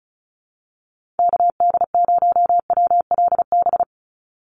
Morse code, heard as a keyed tone, about 36 wpm, 700 Hz. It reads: KD0WLB
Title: USA Call Signs 35wpm